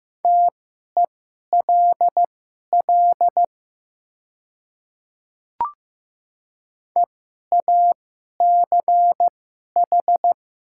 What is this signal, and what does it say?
Morse code, heard as a keyed tone, about 15 wpm, spaced as for 11 wpm, 700 Hz. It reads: TELL EACH
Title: 2x Character Spacing - Top 200 Words - 15wpm